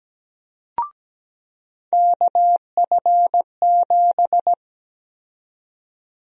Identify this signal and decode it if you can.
Morse code, heard as a keyed tone, about 17 wpm, 700 Hz. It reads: KF7
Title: USA Call Sign Prefixes - 17wpm